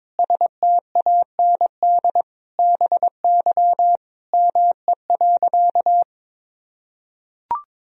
Morse code, heard as a keyed tone, about 22 wpm, 700 Hz. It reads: STAND BY ME.
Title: Sentences from Top 1000 Words - Review Part 1 - 22wpm